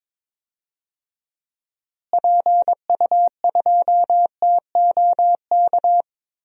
Morse code, heard as a keyed tone, about 22 wpm, 700 Hz. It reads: PU2TOK
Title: DXCC-100 Call Signs - 6 Characters - Repeated 3x - 22wpm